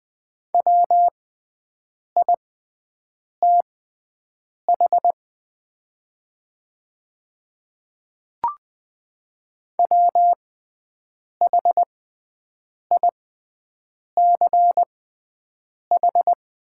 Morse code, heard as a keyed tone, about 20 wpm, spaced as for 7 wpm, 700 Hz. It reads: WITH WHICH
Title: 6x Character Spacing - Top 100 Words - 20wpm